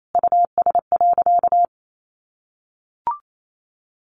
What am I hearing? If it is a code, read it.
Morse code, heard as a keyed tone, about 28 wpm, 700 Hz. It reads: US.
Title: Sentences from Top 1300 Words 28wpm